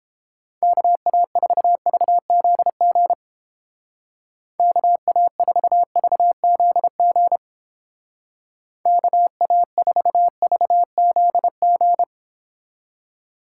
Morse code, heard as a keyed tone, about 31 wpm, 700 Hz. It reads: KA4VZG KA4VZG KA4VZG